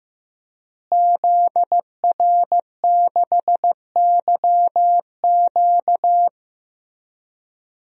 Morse code, heard as a keyed tone, about 15 wpm, 700 Hz. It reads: ZR6YQ
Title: DXCC-100 Call Signs - 5 Characters - Repeated 3x - 15wpm